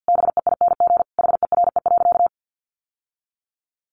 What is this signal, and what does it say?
Morse code, heard as a keyed tone, about 50 wpm, 700 Hz. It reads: BEING HERE.
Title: Sentences from Top 700 Words 50wpm